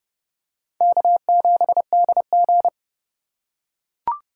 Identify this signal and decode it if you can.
Morse code, heard as a keyed tone, about 30 wpm, 700 Hz. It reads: K7DG
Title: USA Call Signs - 4 Characters - 2x Character Spacing - 30wpm